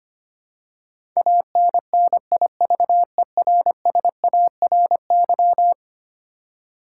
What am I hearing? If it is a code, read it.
Morse code, heard as a keyed tone, about 25 wpm, 700 Hz. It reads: ANNIVERSARY
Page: morse-code-ninja-practice-25wpm